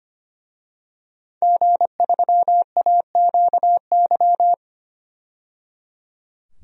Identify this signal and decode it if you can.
Morse code, heard as a keyed tone, about 25 wpm, 700 Hz. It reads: G3AQY
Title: DXCC-100 Call Signs - 5 Characters - 25wpm